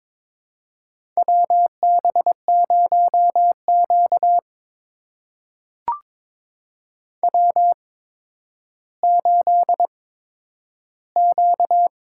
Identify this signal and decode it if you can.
Morse code, heard as a keyed tone, about 23 wpm, spaced as for 11 wpm, 700 Hz. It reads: WB0Q W8Q
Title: USA Call Signs - 4 Characters - 8x Character Spacing - 22wpm